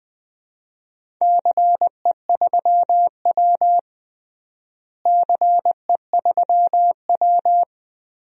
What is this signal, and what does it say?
Morse code, heard as a keyed tone, about 20 wpm, 700 Hz. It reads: CE3W CE3W
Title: DXCC-100 Call Signs - 4 Characters - Repeated 3x - 20wpm